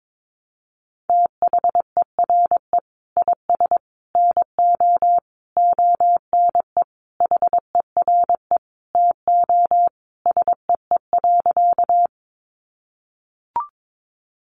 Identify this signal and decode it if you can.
Morse code, heard as a keyed tone, about 22 wpm, 700 Hz. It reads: THERE IS NO ONE HERE TO SEE.
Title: Sentences from Top 200 Words 22wpm